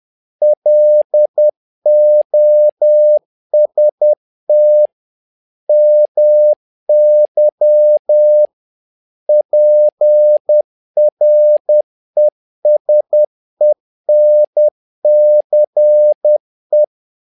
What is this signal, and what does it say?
Morse code, heard as a keyed tone, about 10 wpm, 600 Hz. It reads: LOST MY PRESENCE
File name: learning center / tower trouble / tower trouble 10wpm